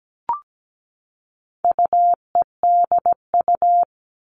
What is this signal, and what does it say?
Morse code, heard as a keyed tone, about 17 wpm, 700 Hz. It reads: UEDU
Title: All Cut Numbers - 4-Digit 17wpm